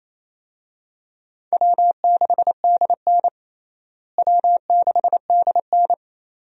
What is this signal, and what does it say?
Morse code, heard as a keyed tone, about 27 wpm, 700 Hz. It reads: W6DN W6DN
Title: USA Call Signs - 4 Characters - Repeated 2x - 28wpm